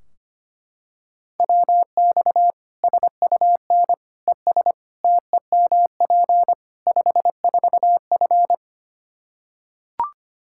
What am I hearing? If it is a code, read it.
Morse code, heard as a keyed tone, about 25 wpm, 700 Hz. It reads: WX SUN ES TEMP 54F
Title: Practice QSO - Exchange 2 - Info - 25wpm